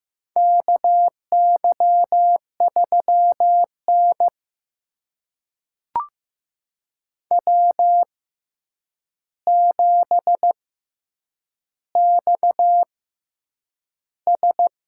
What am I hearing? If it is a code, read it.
Morse code, heard as a keyed tone, about 15 wpm, spaced as for 8 wpm, 700 Hz. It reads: KY3N W7XS
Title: USA Call Signs - 4 Characters - 6x Character Spacing - 15wpm